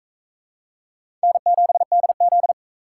Morse code, heard as a keyed tone, about 41 wpm, 700 Hz. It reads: N7DZ